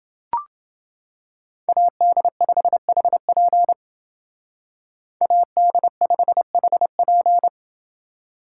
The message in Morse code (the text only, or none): AD5HP AD5HP